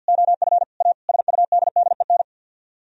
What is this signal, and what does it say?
Morse code, heard as a keyed tone, about 64 wpm, 700 Hz. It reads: OF A SUDDEN